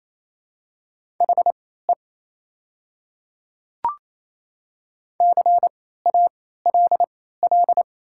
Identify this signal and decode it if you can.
Morse code, heard as a keyed tone, about 28 wpm, spaced as for 16 wpm, 700 Hz. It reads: HE CALL
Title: 3x Character Spacing - Top 100 Words - 28wpm